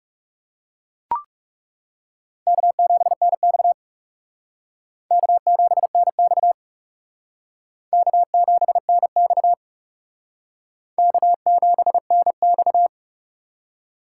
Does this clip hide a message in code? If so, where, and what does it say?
Yes: Morse code from 2.5 to 12.9 s, heard as a keyed tone, about 39 wpm, 700 Hz. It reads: K7NX K7NX K7NX K7NX